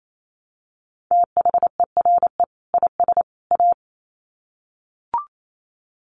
Morse code, heard as a keyed tone, about 28 wpm, 700 Hz. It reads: THERE IS A
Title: Sets of 3 Words 28wpm